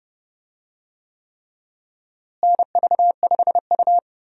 Morse code, heard as a keyed tone, about 31 wpm, 700 Hz. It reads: NV5U
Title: USA Call Signs - 4 Characters - Repeated 3x - 30wpm